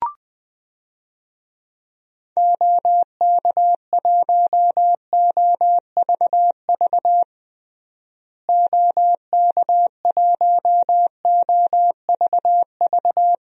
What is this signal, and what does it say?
Morse code, heard as a keyed tone, about 20 wpm, 700 Hz. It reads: OK1OVV OK1OVV
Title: DXCC-100 Call Signs - 6 Characters - Repeated 2x - 20wpm